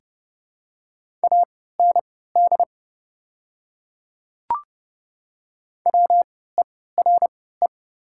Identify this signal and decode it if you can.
Morse code, heard as a keyed tone, about 30 wpm, spaced as for 17 wpm, 700 Hz. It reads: AND WERE
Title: 3x Character Spacing - Top 100 Words - 30wpm